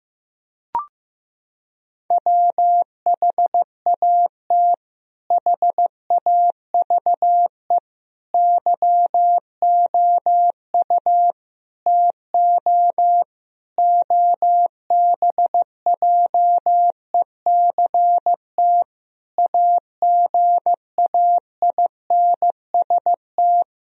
Morse code, heard as a keyed tone, about 15 wpm, 700 Hz. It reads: WHAT HAVE YOU TO OBJECT AGAINST
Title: Sentences from Top 1000 Words - Review Part 3 - 15wpm